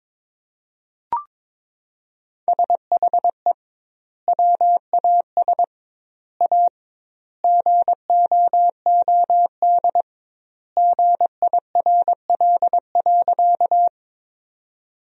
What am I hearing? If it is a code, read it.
Morse code, heard as a keyed tone, about 21 wpm, 700 Hz. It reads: SHE WAS A GOOD GIRL.
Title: 2x Word-Spacing - Sentences from Top 400 Words - 22wpm